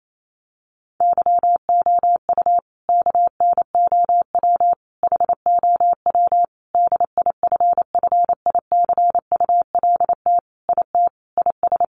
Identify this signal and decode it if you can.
Morse code, heard as a keyed tone, about 28 wpm, 700 Hz. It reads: YOU KNOW HOW DIFFICULT IT IS
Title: Sentences from Top 1100 Words 28wpm